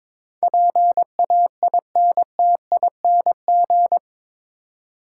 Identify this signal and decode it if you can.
Morse code, heard as a keyed tone, about 22 wpm, 700 Hz. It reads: PAINTING
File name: practice / 22wpm / Eight Letter Words - 22wpm